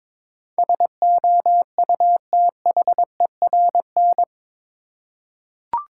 Morse code, heard as a keyed tone, about 21 wpm, 700 Hz. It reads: SOUTHERN
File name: practice / 22wpm / Eight Letter Words - 22wpm